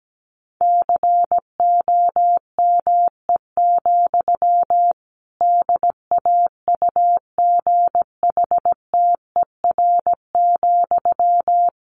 Morse code, heard as a keyed tone, about 17 wpm, 700 Hz. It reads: COME, DAUGHTER,
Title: Sentences from Top 1500 Words 17wpm